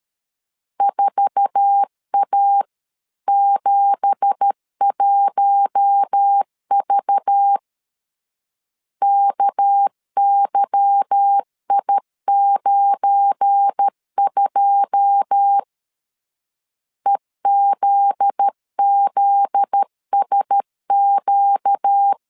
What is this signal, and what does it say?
Morse code, heard as a keyed tone, about 13 wpm, 800 Hz. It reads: 4A71V KYI92 EZZSQ
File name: CW / DET / 1 9 1500C S A 12wpm 3800